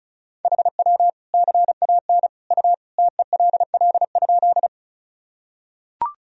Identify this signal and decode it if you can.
Morse code, heard as a keyed tone, about 36 wpm, 700 Hz. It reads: HW CAN U TELL?
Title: Sentences from Top 200 Words and Abbreviations 35wpm